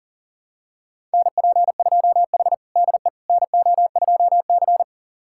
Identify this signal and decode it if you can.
Morse code, heard as a keyed tone, about 40 wpm, 700 Hz. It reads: NP2H DE NO2C
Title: USA Callsign DE Callsign - 4 Characters - 40wpm